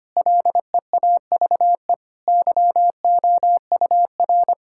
Morse code, heard as a keyed tone, about 25 wpm, 700 Hz. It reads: LEAVE YOUR
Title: Sentences from Top 1600 Words 25wpm